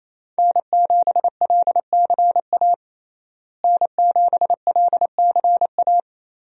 Morse code, heard as a keyed tone, about 27 wpm, 700 Hz. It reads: N7LCA N7LCA
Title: USA Call Signs - 5 Characters - Repeated 2x - 28wpm